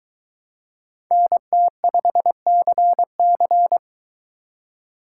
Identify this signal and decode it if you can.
Morse code, heard as a keyed tone, about 22 wpm, 700 Hz. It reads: NT5CC